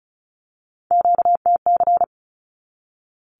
Morse code, heard as a keyed tone, about 34 wpm, 700 Hz. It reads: QTC
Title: Q-Codes 35wpm